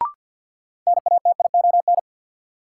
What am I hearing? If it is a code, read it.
Morse code, heard as a keyed tone, about 47 wpm, 700 Hz. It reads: NATION
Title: Sentences from Top 900 Words - New Word Drill - 50wpm